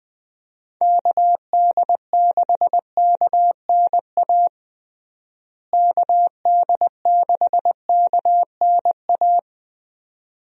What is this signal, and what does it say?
Morse code, heard as a keyed tone, about 20 wpm, 700 Hz. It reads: KD6KNA KD6KNA
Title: USA Call Signs - 6 Characters - Repeated 2x - 20wpm